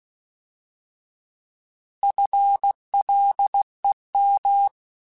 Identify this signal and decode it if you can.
Morse code, heard as a keyed tone, about 16 wpm, 800 Hz. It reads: FLEM
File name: Alphabetical / Alphabetical mp3s / Fs @15